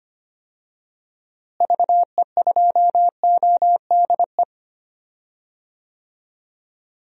Morse code, heard as a keyed tone, about 25 wpm, 700 Hz. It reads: VE2ODE
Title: DXCC-100 Call Signs - 6 Characters - Repeated 2x - 25wpm